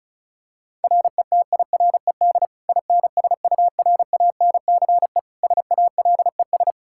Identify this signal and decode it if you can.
Morse code, heard as a keyed tone, about 36 wpm, 700 Hz. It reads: RETIRED INSURANCE SALES
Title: Practice QSO - Exchange 3 - More Info - 35wpm